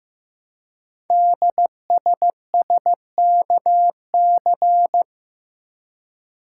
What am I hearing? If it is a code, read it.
Morse code, heard as a keyed tone, about 15 wpm, 700 Hz. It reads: DSSKC
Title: Code Groups - 5 Letters - 15wpm